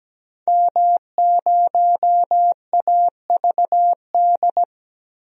Morse code, heard as a keyed tone, about 17 wpm, 700 Hz. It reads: M0AVD
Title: DXCC-100 Call Signs - 5 Characters - Repeated 2x - 17wpm